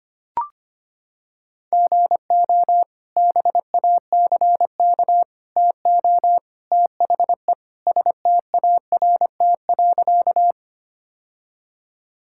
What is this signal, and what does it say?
Morse code, heard as a keyed tone, about 25 wpm, 700 Hz. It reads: GO BACK TO THE START.
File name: practice / 25wpm / Sentences with 3-Word Phrases - 25wpm